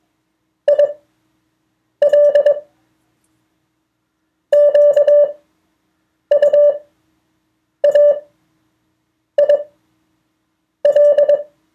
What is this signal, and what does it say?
Morse code, heard as a keyed tone, about 22 wpm, spaced as for 6 wpm, 580 Hz. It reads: IL QUAIL
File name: 2of4-words-1